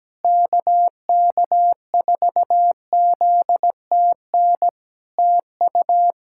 Morse code, heard as a keyed tone, about 17 wpm, 700 Hz. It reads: KK4ZTN TU
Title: Practice QSO - Exchange 4 - Ending - 17wpm